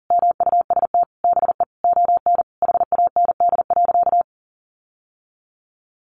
Morse code, heard as a keyed tone, about 40 wpm, 700 Hz. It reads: MUST BE ON HAND.
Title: Sentences from Top 1500 Words 40wpm